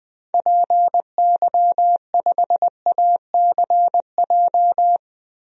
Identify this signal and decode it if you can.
Morse code, heard as a keyed tone, about 20 wpm, 700 Hz. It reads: PY5ACJ
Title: DXCC-100 Call Signs - 6 Characters - 20wpm